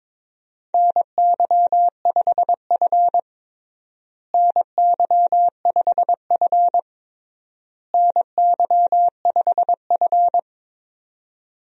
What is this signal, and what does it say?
Morse code, heard as a keyed tone, about 22 wpm, 700 Hz. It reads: NY5F NY5F NY5F
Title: USA Call Signs - 4 Characters - Repeated 3x - 22wpm